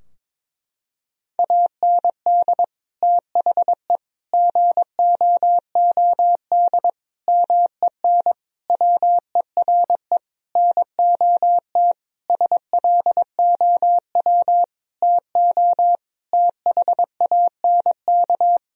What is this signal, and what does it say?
Morse code, heard as a keyed tone, about 22 wpm, 700 Hz. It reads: AND THE GOOD MEN WERE NOT SLOW TO THANK
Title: Sentences from Top 1000 Words - Review Part 2 - 22wpm